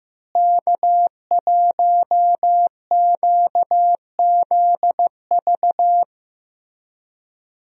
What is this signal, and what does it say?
Morse code, heard as a keyed tone, about 15 wpm, 700 Hz. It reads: K1QZV
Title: USA Call Signs - 5 Characters - Repeated 2x - 15wpm